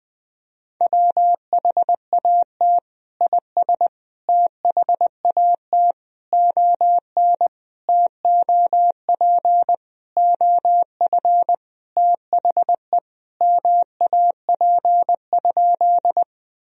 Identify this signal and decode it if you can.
Morse code, heard as a keyed tone, about 20 wpm, 700 Hz. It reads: WHAT IS THAT ON TOP OF THE MAP?
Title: Sentences from Top 1000 Words - Review Part 2 - 20wpm